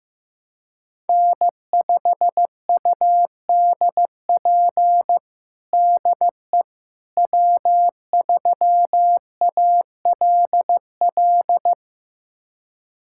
Morse code, heard as a keyed tone, about 15 wpm, 700 Hz. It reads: N5UDP DE W3ALL
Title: USA Callsign DE Callsign - 5 Characters - 15wpm